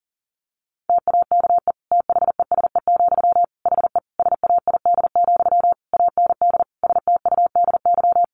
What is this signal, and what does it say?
Morse code, heard as a keyed tone, about 40 wpm, 700 Hz. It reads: TAKE THESE, HE SAID, AND STUDY
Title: Sentences from Top 300 Words 40wpm